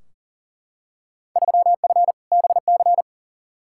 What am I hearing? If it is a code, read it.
Morse code, heard as a keyed tone, about 40 wpm, 700 Hz. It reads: 3F BC
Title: Field Day Exchanges Encore - 40wpm